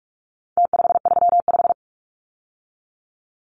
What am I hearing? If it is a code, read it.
Morse code, heard as a keyed tone, about 44 wpm, 700 Hz. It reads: T535
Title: Numbers - Contest Style - 4-Digit 45wpm